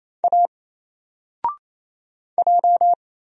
Morse code, heard as a keyed tone, about 27 wpm, 700 Hz. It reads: A J
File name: Single-Letter-Number-28wpm